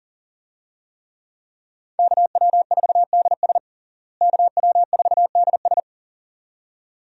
Morse code, heard as a keyed tone, about 40 wpm, 700 Hz. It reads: KW4DS KW4DS
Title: USA Call Signs - 5 Characters - Repeated 2x - 40wpm